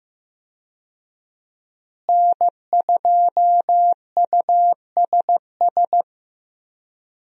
Morse code, heard as a keyed tone, about 15 wpm, 700 Hz. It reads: N2USS
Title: USA Call Signs - 5 Characters - Repeated 2x - 15wpm